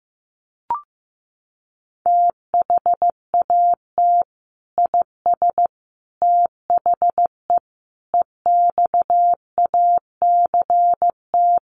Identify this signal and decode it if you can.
Morse code, heard as a keyed tone, about 15 wpm, 700 Hz. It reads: THAT IS THE EXACT
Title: Sentences from Top 1300 Words 15wpm